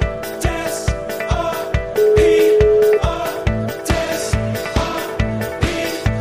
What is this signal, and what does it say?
Ringback tone (Europe, 425 Hz), with noise.